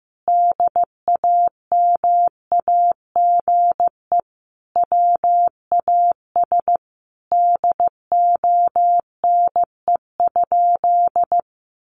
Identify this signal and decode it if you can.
Morse code, heard as a keyed tone, about 14 wpm, 700 Hz. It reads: DAMAGE WAS DONE?
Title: Sentences from Top 1100 Words 15wpm